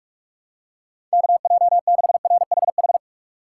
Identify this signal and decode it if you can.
Morse code, heard as a keyed tone, about 46 wpm, 700 Hz. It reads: KJ6RHH
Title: USA Call Signs - 6 Characters - 6x Character Spacing - 45wpm